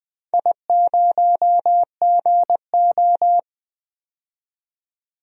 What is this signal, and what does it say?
Morse code, heard as a keyed tone, about 21 wpm, 700 Hz. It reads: I0GO
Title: DXCC-100 Call Signs - 4 Characters - 20wpm